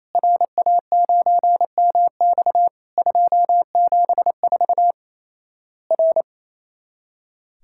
Morse code, heard as a keyed tone, about 28 wpm, 700 Hz. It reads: RA9MX 274 R
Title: Worked All Europe DX Contest - QTC Exchange - 28wpm